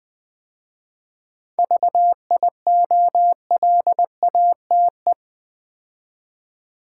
Morse code, heard as a keyed tone, about 20 wpm, 700 Hz. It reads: VIOLATE
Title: 7 Letter Words - First 3 Letters Given - 20wpm